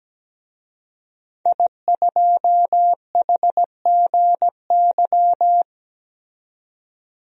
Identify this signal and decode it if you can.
Morse code, heard as a keyed tone, about 17 wpm, 700 Hz. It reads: I2HGY
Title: DXCC-100 Call Signs - 5 Characters - 17wpm